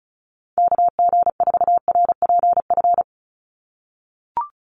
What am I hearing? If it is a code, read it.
Morse code, heard as a keyed tone, about 34 wpm, 700 Hz. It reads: KG4RPF
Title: USA Call Signs 35wpm